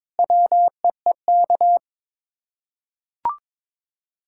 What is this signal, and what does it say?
Morse code, heard as a keyed tone, about 22 wpm, 700 Hz. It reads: WEEK
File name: practice / 22wpm / Sets of 7 Words - 22wpm